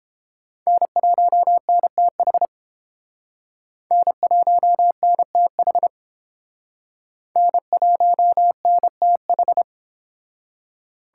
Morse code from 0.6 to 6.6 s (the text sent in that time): N1NTH N1NTH